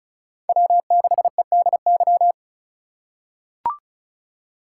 Morse code, heard as a keyed tone, about 36 wpm, 700 Hz. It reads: W6EDY
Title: K1USN SST QSO Practice - 35wpm